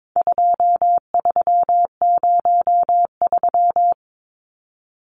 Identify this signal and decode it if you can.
Morse code, heard as a keyed tone, about 22 wpm, 700 Hz. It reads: 2303
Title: Numbers - 4-Digit 22wpm